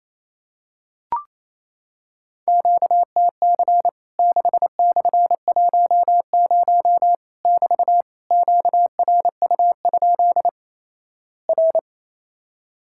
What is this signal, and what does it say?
Morse code, heard as a keyed tone, about 28 wpm, 700 Hz. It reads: QTC 6/10 <BT> QRU? R
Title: Worked All Europe DX Contest - QTC Exchange - 28wpm